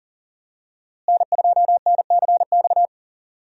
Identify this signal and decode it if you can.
Morse code, heard as a keyed tone, about 37 wpm, 700 Hz. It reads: N2NCX
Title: USA Call Signs - 5 Characters - Repeated 2x - 40wpm